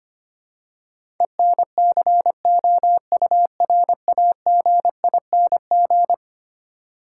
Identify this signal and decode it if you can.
Morse code, heard as a keyed tone, about 25 wpm, 700 Hz. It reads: ENCOURAGING
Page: morse-code-ninja-practice-25wpm